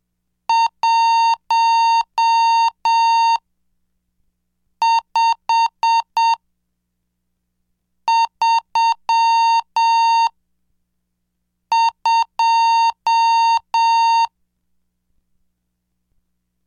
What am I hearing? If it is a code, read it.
Morse code, heard as a keyed tone, about 7 wpm, spaced as for 4 wpm, 920 Hz. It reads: 1532
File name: cursos / CDestudo / 24 Track 24